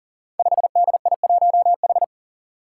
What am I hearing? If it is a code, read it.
Morse code, heard as a keyed tone, about 40 wpm, 700 Hz. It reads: 5DI1H
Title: Five Challenging Letter-Number Combinations 40wpm